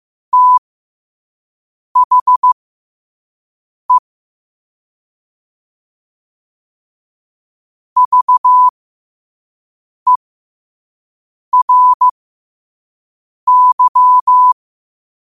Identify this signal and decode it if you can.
Morse code, heard as a keyed tone, about 15 wpm, spaced as for 5 wpm, 1000 Hz. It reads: THE VERY
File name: qotdfiles / 5wpm